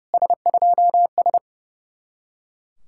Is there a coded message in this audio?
Yes: Morse code from 0.1 to 1.4 s, heard as a keyed tone, about 40 wpm, 700 Hz. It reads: S2S